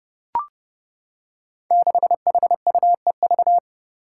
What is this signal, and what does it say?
Morse code, heard as a keyed tone, about 30 wpm, 700 Hz. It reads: 6HUEV
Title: Five Challenging Letter-Number Combinations 30wpm